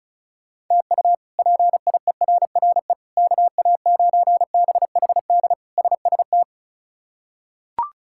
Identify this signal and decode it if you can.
Morse code, heard as a keyed tone, about 36 wpm, 700 Hz. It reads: TU PIERRE KA9BHD SST
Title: K1USN SST QSO Practice - 35wpm